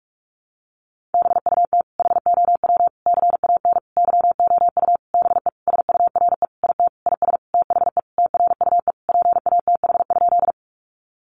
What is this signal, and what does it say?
Morse code, heard as a keyed tone, about 44 wpm, 700 Hz. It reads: BUT HOW CAN YOU BE SURE IT IS THE TRUE PATH?
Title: Sentences from Top 1400 Words 45wpm